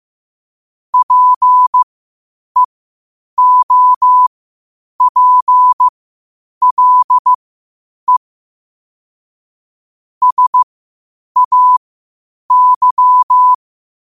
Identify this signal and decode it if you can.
Morse code, heard as a keyed tone, about 15 wpm, spaced as for 8 wpm, 1000 Hz. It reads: PEOPLE SAY
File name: qotdfiles / 8wpm